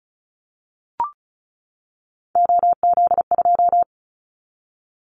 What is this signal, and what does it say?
Morse code, heard as a keyed tone, about 34 wpm, 700 Hz. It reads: OZ2
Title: Callsign Prefixes for Top 100 DXCC - 35wpm